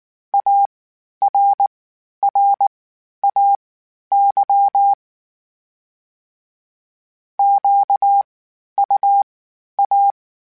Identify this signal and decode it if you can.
Morse code, heard as a keyed tone, about 19 wpm, spaced as for 11 wpm, 800 Hz. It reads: ARRAY QUA
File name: (spaced) / FromSMs spaced mp3s / Antennas sp@18